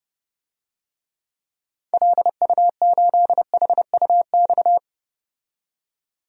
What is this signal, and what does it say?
Morse code, heard as a keyed tone, about 30 wpm, 700 Hz. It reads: LU8HUX
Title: DXCC-100 Call Signs - 6 Characters - 30wpm